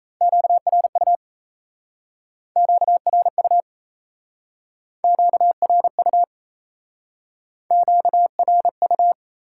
Morse code, heard as a keyed tone, about 37 wpm, 700 Hz. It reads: QRU QRU QRU QRU